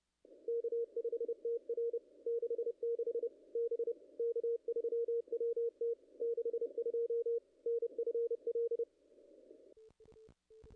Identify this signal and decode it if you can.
Morse code, heard as a keyed tone, about 28 wpm, 440 Hz. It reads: K5TR 66 B K3WT 62 NFL
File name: contests / ss-cw-k5tr-op-n5rz -0000